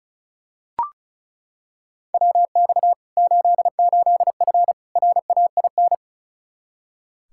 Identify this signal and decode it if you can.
Morse code, heard as a keyed tone, about 36 wpm, 700 Hz. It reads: WX 88F RAIN
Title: Practice QSO - Exchange 2 - Info - 35wpm